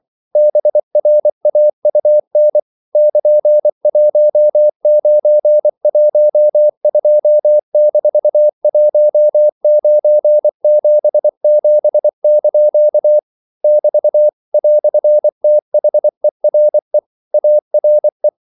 Morse code, heard as a keyed tone, about 24 wpm, 600 Hz. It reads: BRAUN <KN>1912-1977) <BT> "THERE ARE
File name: CW / Practice / EnglishQuotations0424